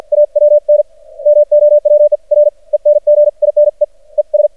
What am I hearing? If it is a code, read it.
Morse code, heard as a keyed tone, about 50 wpm, spaced as for 39 wpm, 590 Hz. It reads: AWN MOWER ENGINEES
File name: archived / CRSnet